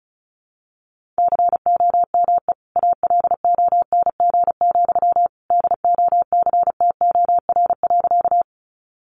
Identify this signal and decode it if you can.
Morse code, heard as a keyed tone, about 34 wpm, 700 Hz. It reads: COME ALONG, DOCTOR.
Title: Sentences from Top 1500 Words 35wpm